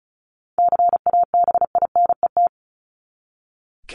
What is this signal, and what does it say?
Morse code, heard as a keyed tone, about 34 wpm, 700 Hz. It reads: CABINET
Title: Top 1751-2000 Words 35wpm